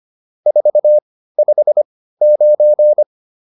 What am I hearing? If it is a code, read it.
Morse code, heard as a keyed tone, about 25 wpm, spaced as for 15 wpm, 600 Hz. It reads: 459